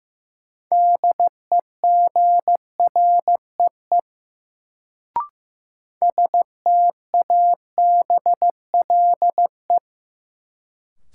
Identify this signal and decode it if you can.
Morse code, heard as a keyed tone, about 15 wpm, 700 Hz. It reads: DEGREE STABLE
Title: Sentences from Top 1500 Words - New Word Drill Alt - 15wpm